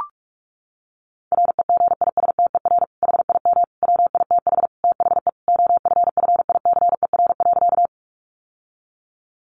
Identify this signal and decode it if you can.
Morse code, heard as a keyed tone, about 44 wpm, 700 Hz. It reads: REGISTER HIM WITH THE OFFICER.
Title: Sentences from Top 1200 Words 45wpm